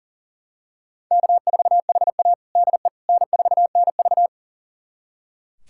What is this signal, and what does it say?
Morse code, heard as a keyed tone, about 40 wpm, 700 Hz. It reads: K4HA DE N4NV
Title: USA Callsign DE Callsign - 4 Characters - 40wpm